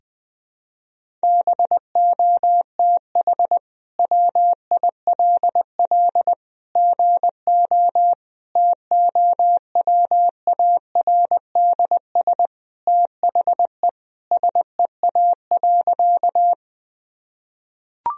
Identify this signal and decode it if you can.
Morse code, heard as a keyed tone, about 20 wpm, 700 Hz. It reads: BOTH WILL GO TOWARDS THE SEA.
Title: Sentences from Top 1000 Words - Review Part 2 - 20wpm